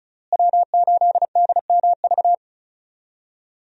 Morse code, heard as a keyed tone, about 36 wpm, 700 Hz. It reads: W8DMV
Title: USA Call Signs - 5 Characters - Repeated 2x - 35wpm